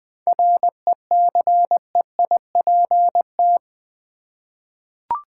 Morse code, heard as a keyed tone, about 20 wpm, 700 Hz. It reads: RECEIPT
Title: 7 Letter Words - First 3 Letters Given - 20wpm